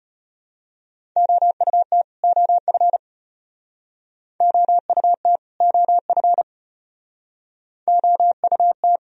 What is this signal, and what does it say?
Morse code, heard as a keyed tone, about 35 wpm, 700 Hz. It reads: OUT OF OUT OF OUT